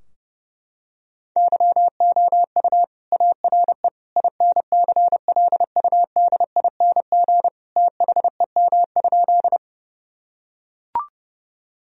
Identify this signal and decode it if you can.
Morse code, heard as a keyed tone, about 32 wpm, 700 Hz. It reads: YOU ARE INCLUDING THEM?
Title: Sentences from Top 1000 Words - Review Part 4 - 30wpm